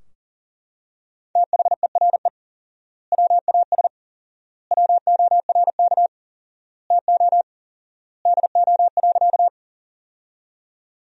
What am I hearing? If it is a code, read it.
Morse code, heard as a keyed tone, about 37 wpm, 700 Hz. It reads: THERE WAS WORK TO DO.
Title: 4x Word-Spacing - Sentences from Top 200 Words - 40wpm